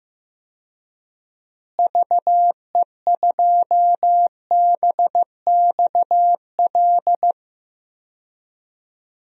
Morse code, heard as a keyed tone, about 15 wpm, 700 Hz. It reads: VE2BXL
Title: DXCC-100 Call Signs - 6 Characters - Repeated 2x - 15wpm